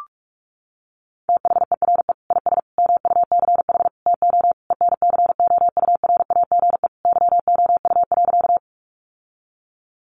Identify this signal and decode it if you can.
Morse code, heard as a keyed tone, about 44 wpm, 700 Hz. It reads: THERE IS MUCH TO ENCOURAGE YOU.
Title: Sentences from Top 1400 Words 45wpm